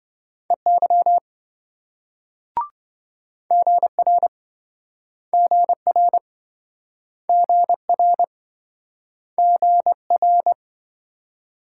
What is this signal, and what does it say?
Morse code, heard as a keyed tone, about 27 wpm, 700 Hz. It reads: EY GR GR GR GR